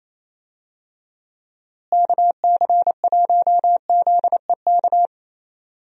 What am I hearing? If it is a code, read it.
Morse code, heard as a keyed tone, about 28 wpm, 700 Hz. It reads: KC1ZEK
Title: USA Call Signs - 6 Characters - Repeated 3x - 28wpm